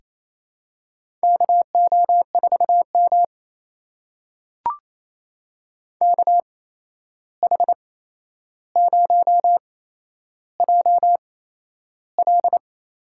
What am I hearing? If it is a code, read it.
Morse code, heard as a keyed tone, about 27 wpm, spaced as for 11 wpm, 700 Hz. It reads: KO4M KH0JL
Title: USA Call Signs - 8x Character Spacing - 28wpm